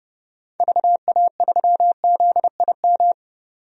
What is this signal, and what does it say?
Morse code, heard as a keyed tone, about 32 wpm, 700 Hz. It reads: VA3ZIM
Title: DXCC-100 Call Signs - 6 Characters - 30wpm